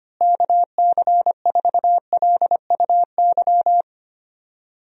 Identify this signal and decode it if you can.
Morse code, heard as a keyed tone, about 25 wpm, 700 Hz. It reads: KC4LUY
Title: USA Call Signs - 6 Characters - Repeated 3x - 25wpm